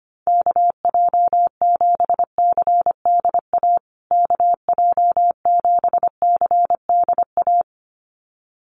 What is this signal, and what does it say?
Morse code, heard as a keyed tone, about 25 wpm, 700 Hz. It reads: KJ7CDA KJ7CDA
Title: USA Call Signs Repeated 2x 25wpm